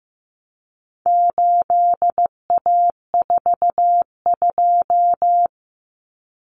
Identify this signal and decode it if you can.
Morse code, heard as a keyed tone, about 15 wpm, 700 Hz. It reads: 8A42
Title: Common Cut Numbers - 4-Digit 15wpm